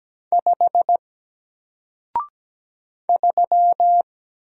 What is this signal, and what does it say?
Morse code, heard as a keyed tone, about 17 wpm, 700 Hz. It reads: 5 3
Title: Single Letter-Number Alt 17wpm